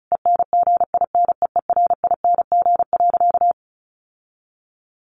Morse code, heard as a keyed tone, about 34 wpm, 700 Hz. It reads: ENGINEERING.
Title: Sentences from Top 1500 Words 35wpm